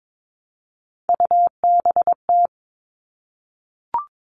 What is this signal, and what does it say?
Morse code, heard as a keyed tone, about 22 wpm, 700 Hz. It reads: UBT
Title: All Cut Numbers - 3-Digit 22wpm